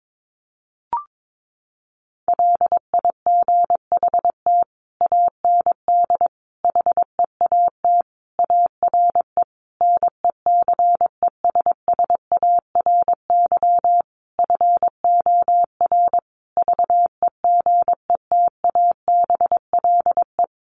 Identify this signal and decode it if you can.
Morse code, heard as a keyed tone, about 22 wpm, 700 Hz. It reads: LIGHT AND HEAT ARE NECESSARY FOR VEGETABLE
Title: Sentences from Top 1300 Words 22wpm